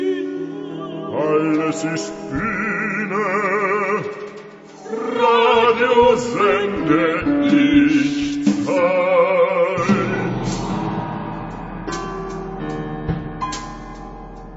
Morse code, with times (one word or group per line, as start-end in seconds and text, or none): none